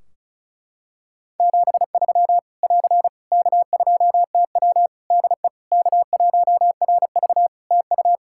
Morse code, heard as a keyed tone, about 34 wpm, 700 Hz. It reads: 73 <AR> K2TW DE K1RV TU
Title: K1USN SST QSO Practice 35wpm